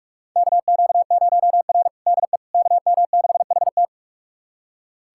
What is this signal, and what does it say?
Morse code, heard as a keyed tone, about 43 wpm, 700 Hz. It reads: KQ0R DE KM6HT
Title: USA Callsign DE Callsign 45wpm